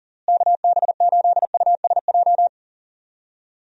Morse code, heard as a keyed tone, about 40 wpm, 700 Hz. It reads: KB8USJ
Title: USA Call Signs - 6 Characters - Repeated 3x - 40wpm